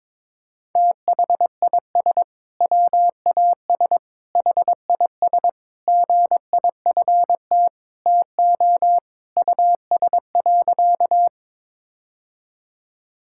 Morse code, heard as a keyed tone, about 22 wpm, 700 Hz. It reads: THIS WAS HIS GIFT TO US.
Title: Sentences from Top 1000 Words - Review Part 6 - 22wpm